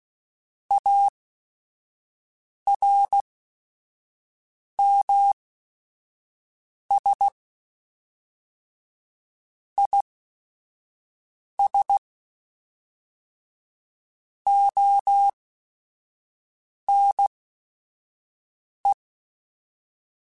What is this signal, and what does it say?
Morse code, heard as a keyed tone, about 16 wpm, spaced as for 5 wpm, 800 Hz. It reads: ARMS IS ONE
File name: Quotes / Quotes-mp3s / ChildAsleep @5